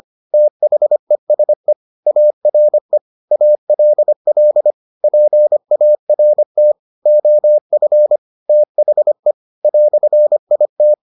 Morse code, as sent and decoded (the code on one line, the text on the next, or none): - .... ... / .- .-. / .- .-.. .-.. / .--. .- .-. - / --- ..-. / - .... / .-..-. .. -
THESE ARE ALL PART OF THE "IT